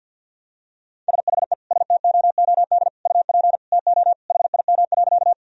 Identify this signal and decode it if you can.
Morse code, heard as a keyed tone, about 46 wpm, 700 Hz. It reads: SHE STOOD UP TO HIM.
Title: Sentences from Top 1000 Words - Review Part 2 - 50wpm